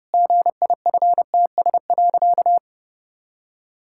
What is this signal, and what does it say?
Morse code, heard as a keyed tone, about 32 wpm, 700 Hz. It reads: GIFTS.
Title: Pangram Challenge v2 - 30wpm